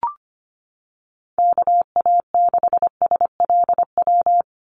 Morse code, heard as a keyed tone, about 25 wpm, 700 Hz. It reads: KA6SLW
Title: USA Call Signs 25wpm